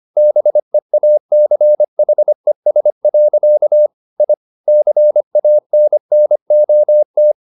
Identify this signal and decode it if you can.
Morse code, heard as a keyed tone, about 25 wpm, 600 Hz. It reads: BEACHES. I CANNOT